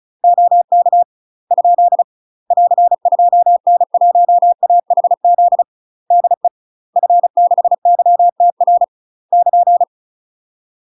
Morse code, heard as a keyed tone, about 34 wpm, 700 Hz. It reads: OK ? <AR> 2N1AHZ DE F6YTR <KN>
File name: UK MASTER QSOs TRAINING / 35WPM / UK MASTER QSOs 0039